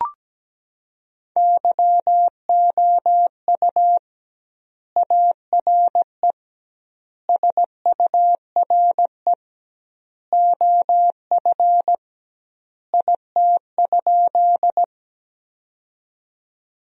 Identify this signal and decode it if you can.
Morse code, heard as a keyed tone, about 17 wpm, 700 Hz. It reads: YOU ARE SURE OF IT?
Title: 2x Word-Spacing - Sentences from Top 300 Words - 17wpm